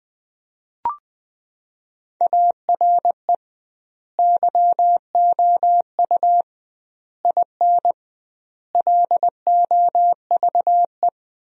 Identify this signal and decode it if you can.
Morse code, heard as a keyed tone, about 21 wpm, 700 Hz. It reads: ARE YOU IN LOVE
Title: 2x Word-Spacing - Sentences from Top 400 Words - 20wpm